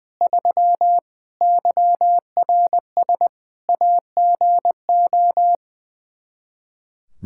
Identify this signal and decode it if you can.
Morse code, heard as a keyed tone, about 20 wpm, 700 Hz. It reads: 3 YRS AGO
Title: Practice QSO - Exchange 3 - More Info - 20wpm